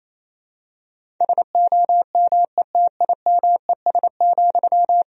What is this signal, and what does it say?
Morse code, heard as a keyed tone, about 27 wpm, 700 Hz. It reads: SOMETIMES,
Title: Sentences from Top 1000 Words - Review Part 3 - 28wpm